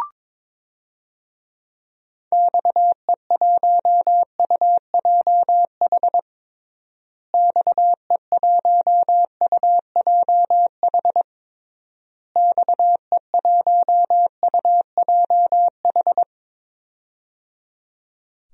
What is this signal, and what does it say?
Morse code, heard as a keyed tone, about 22 wpm, 700 Hz. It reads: XE1UJH XE1UJH XE1UJH
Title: DXCC-100 Call Signs - 6 Characters - Repeated 3x - 22wpm